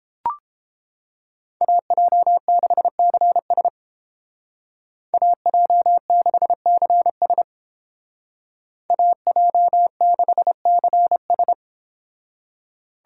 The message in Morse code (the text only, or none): AJ6CS AJ6CS AJ6CS